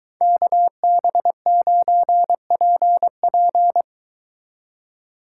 Morse code, heard as a keyed tone, about 23 wpm, 700 Hz. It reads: KB9PP